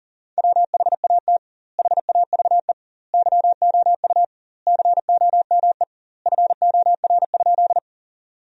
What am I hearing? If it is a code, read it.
Morse code, heard as a keyed tone, about 40 wpm, 700 Hz. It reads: WHAT HAVE YOU COME FOR?
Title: 2x Word-Spacing - Sentences from Top 100 Words - 40wpm